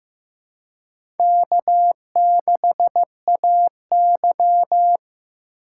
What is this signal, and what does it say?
Morse code, heard as a keyed tone, about 15 wpm, 700 Hz. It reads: K6AY
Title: USA Call Signs - 4 Characters - 4x Character Spacing - 15wpm